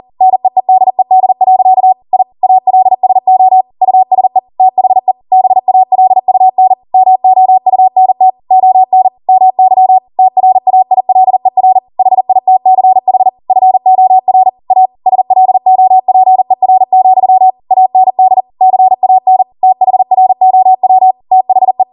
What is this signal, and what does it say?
Morse code, heard as a keyed tone, about 40 wpm, 750 Hz. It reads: NEEDED. I ALSO USE THE BALUN MOUNT ON MY TRAILER HITCH FOR A SLOPER, AND CAN THROW THE